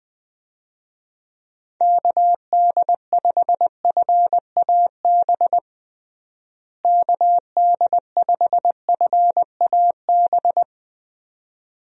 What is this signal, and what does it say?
Morse code, heard as a keyed tone, about 20 wpm, 700 Hz. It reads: KD5FAB KD5FAB
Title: USA Call Signs - 6 Characters - Repeated 2x - 20wpm